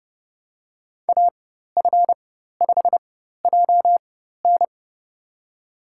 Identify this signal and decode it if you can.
Morse code, heard as a keyed tone, about 31 wpm, spaced as for 14 wpm, 700 Hz. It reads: AF5JN
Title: USA Call Signs - 4x Character Spacing - 30wpm